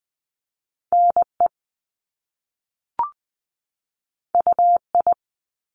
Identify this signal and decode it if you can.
Morse code, heard as a keyed tone, about 20 wpm, 700 Hz. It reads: NE UI
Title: Top Two Letter Combinations in Words 20wpm